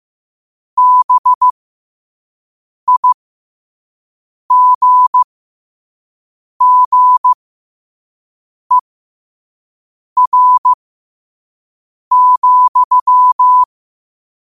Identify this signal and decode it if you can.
Morse code, heard as a keyed tone, about 15 wpm, spaced as for 5 wpm, 1000 Hz. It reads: BIGGER,
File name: qotdfiles / 5wpm